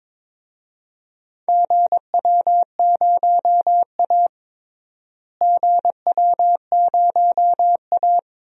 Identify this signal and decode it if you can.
Morse code, heard as a keyed tone, about 23 wpm, 700 Hz. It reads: GW0A GW0A
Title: DXCC-100 Call Signs - 4 Characters - Repeated 2x - 22wpm